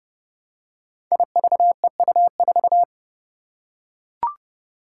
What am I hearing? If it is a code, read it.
Morse code, heard as a keyed tone, about 30 wpm, 700 Hz. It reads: IVEU4
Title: Five Challenging Letter-Number Combinations 30wpm